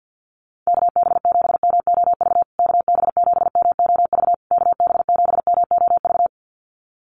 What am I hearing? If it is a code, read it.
Morse code, heard as a keyed tone, about 50 wpm, 700 Hz. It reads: KB7MOV KB7MOV KB7MOV
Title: USA Call Signs Repeated 3x 50wpm